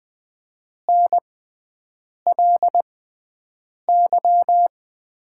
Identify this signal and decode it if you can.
Morse code, heard as a keyed tone, about 20 wpm, spaced as for 7 wpm, 700 Hz. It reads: NLY
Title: 6x Character Spacing - Top 200 Words - 20wpm